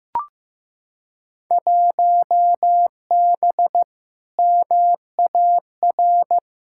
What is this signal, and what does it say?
Morse code, heard as a keyed tone, about 15 wpm, 700 Hz. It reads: 1B MAR
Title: Field Day Exchanges Encore - 15wpm